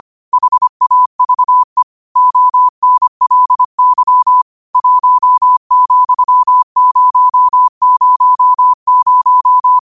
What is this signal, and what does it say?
Morse code, heard as a keyed tone, about 25 wpm, 1000 Hz. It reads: HAVE ONLY 1,000
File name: qotdfiles / 25wpm